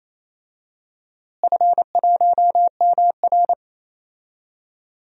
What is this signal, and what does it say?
Morse code, heard as a keyed tone, about 28 wpm, 700 Hz. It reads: F1MR
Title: DXCC-100 Call Signs - 4 Characters - Repeated 2x - 28wpm